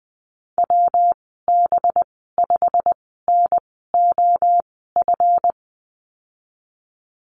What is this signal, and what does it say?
Morse code, heard as a keyed tone, about 20 wpm, spaced as for 15 wpm, 700 Hz. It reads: WB5NOF
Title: 186 Lesson 30g - Letter b - Call Signs 20wpm